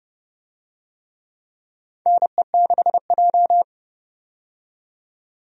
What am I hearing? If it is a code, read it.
Morse code, heard as a keyed tone, about 32 wpm, 700 Hz. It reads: NE6J
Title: USA Call Signs - 4 Characters - 30wpm